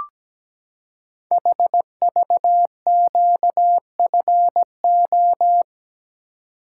Morse code, heard as a keyed tone, about 17 wpm, 700 Hz. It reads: HVQFO
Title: Code Groups - 5 Letters - 17wpm